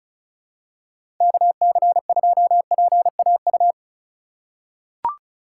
Morse code, heard as a keyed tone, about 34 wpm, 700 Hz. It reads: KC2PAU
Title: USA Call Signs - 6 Characters - 2x Character Spacing - 35wpm